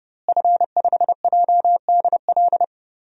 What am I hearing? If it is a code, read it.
Morse code, heard as a keyed tone, about 30 wpm, 700 Hz. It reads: F5JDL
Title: DXCC-100 Call Signs - 5 Characters - Repeated 2x - 30wpm